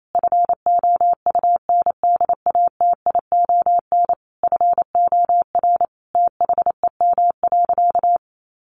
Morse code, heard as a keyed tone, about 28 wpm, 700 Hz. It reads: FOUNDATION FOR THEM.
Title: Sentences from Top 1200 Words 28wpm